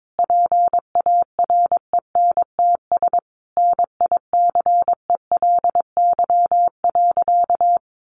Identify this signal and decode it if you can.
Morse code, heard as a keyed tone, about 22 wpm, 700 Hz. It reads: PARENTS NICELY.
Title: Easy Phrases 22wpm